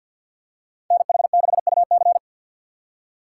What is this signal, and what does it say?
Morse code, heard as a keyed tone, about 54 wpm, 700 Hz. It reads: NH6UC
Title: USA Call Signs - 4x Character Spacing - 50wpm